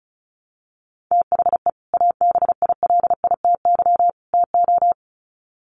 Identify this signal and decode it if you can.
Morse code, heard as a keyed tone, about 34 wpm, 700 Hz. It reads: THE ABILITY TO
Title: Sets of 3 Words 35wpm